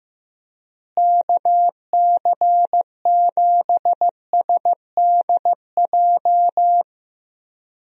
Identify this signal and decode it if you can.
Morse code, heard as a keyed tone, about 15 wpm, 700 Hz. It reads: KC7SDJ
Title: USA Call Signs - 6 Characters - Repeated 3x - 15wpm